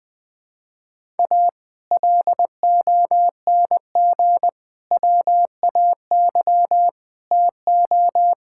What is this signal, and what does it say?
Morse code, heard as a keyed tone, about 21 wpm, 700 Hz. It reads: A LONG WAY TO
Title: Build A Phrase - 5-Word Phrases - 20wpm